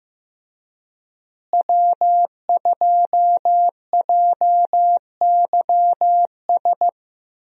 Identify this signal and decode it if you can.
Morse code, heard as a keyed tone, about 15 wpm, 700 Hz. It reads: W2JYS
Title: USA Call Signs - 5 Characters - Repeated 2x - 15wpm